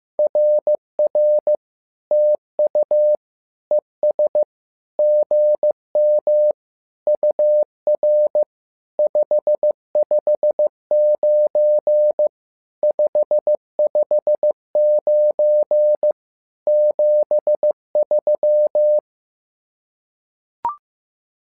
Morse code, heard as a keyed tone, about 15 wpm, 600 Hz. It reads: RR TU ES GM UR 559 559 73
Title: SOTA QSO Practice 15wpm